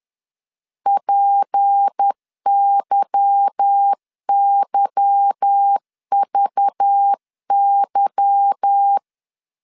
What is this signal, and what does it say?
Morse code, heard as a keyed tone, about 10 wpm, 800 Hz. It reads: PYYVY
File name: LEZIONI / 8 YJ45 / 8 Y 10wpm 2615